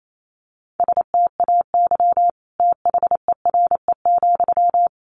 Morse code, heard as a keyed tone, about 28 wpm, 700 Hz. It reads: STAY THERE,
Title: Sentences from Top 1200 Words 28wpm